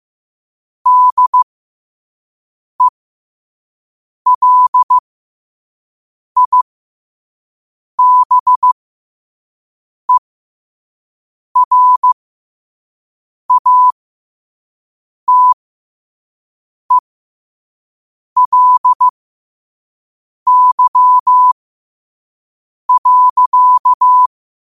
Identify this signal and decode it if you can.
Morse code, heard as a keyed tone, about 15 wpm, spaced as for 5 wpm, 1000 Hz. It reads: DELIBERATELY.
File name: qotdfiles / 5wpm